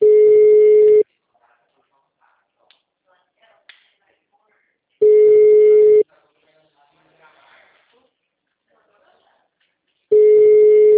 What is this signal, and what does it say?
Ringback tone (Europe, 425 Hz).